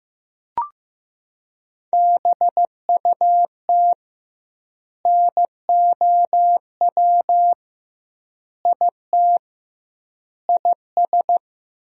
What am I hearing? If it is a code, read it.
Morse code, heard as a keyed tone, about 15 wpm, 700 Hz. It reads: BUT NOW IT IS
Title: 2x Word-Spacing - Sentences from Top 300 Words - 15wpm